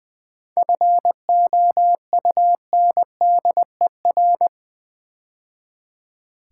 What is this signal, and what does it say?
Morse code, heard as a keyed tone, about 20 wpm, 700 Hz. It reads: FOUNDER
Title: 7 Letter Words - First 3 Letters Given - 20wpm